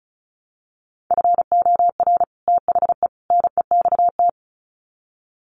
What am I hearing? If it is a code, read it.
Morse code, heard as a keyed tone, about 34 wpm, 700 Hz. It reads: FOR THE NEXT
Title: Sets of 3 Words 35wpm